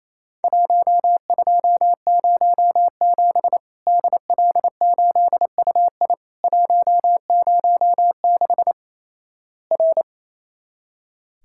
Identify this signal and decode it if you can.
Morse code, heard as a keyed tone, about 28 wpm, 700 Hz. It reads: 1207 DL8UI 106 R
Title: Worked All Europe DX Contest - QTC Exchange - 28wpm